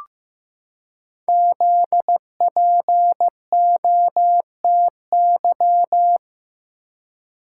Morse code, heard as a keyed tone, about 15 wpm, 700 Hz. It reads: ZPOTY
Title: Code Groups - 5 Letters - 15wpm